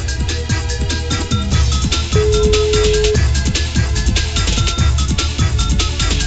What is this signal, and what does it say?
Ringback tone (Europe, 425 Hz), with noise.